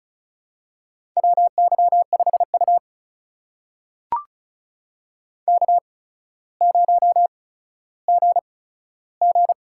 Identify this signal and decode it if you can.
Morse code, heard as a keyed tone, about 36 wpm, spaced as for 15 wpm, 700 Hz. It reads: WY5U K0GG
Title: USA Call Signs - 4 Characters - 8x Character Spacing - 35wpm